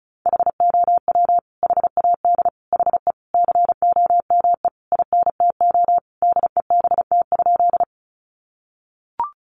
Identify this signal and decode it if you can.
Morse code, heard as a keyed tone, about 34 wpm, 700 Hz. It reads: HOW HAD HE COME INTO DEBT?
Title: Sentences from Top 1300 Words 35wpm